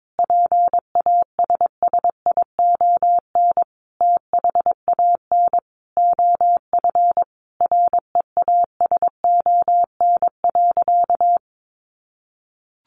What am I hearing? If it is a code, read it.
Morse code, heard as a keyed tone, about 22 wpm, 700 Hz. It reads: PASSION THAN OF REASON.
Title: Sentences from Top 1300 Words 22wpm